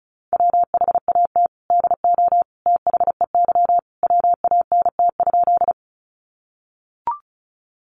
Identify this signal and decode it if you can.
Morse code, heard as a keyed tone, about 34 wpm, 700 Hz. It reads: WHAT DO THEY WANT?
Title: Sentences from Top 200 Words 35wpm